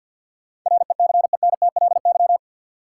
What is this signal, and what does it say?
Morse code, heard as a keyed tone, about 52 wpm, 700 Hz. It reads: RECENTLY
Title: Sentences from Top 1200 Words - New Word Drill Alt - 50wpm